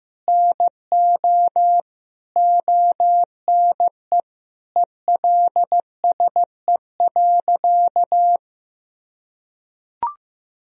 Morse code, heard as a keyed tone, about 15 wpm, 700 Hz. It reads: NO ONE ELSE.
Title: Sentences from Top 1000 Words - Review Part 1 - 15wpm